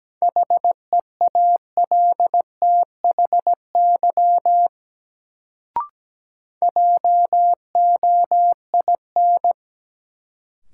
Morse code, heard as a keyed tone, about 17 wpm, 700 Hz. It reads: HEALTHY JOIN
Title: Sentences from Top 1400 Words - New Word Drill Alt - 17wpm